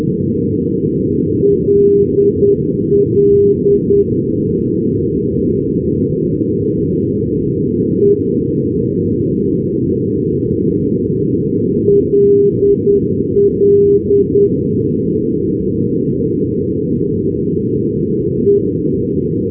Morse code, heard as a keyed tone, about 10 wpm, spaced as for 3 wpm, 400 Hz, with noise. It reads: LLE LLE
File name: europe / f-2654 ndb lille ll(e)-332-np-f-paris-2012-rb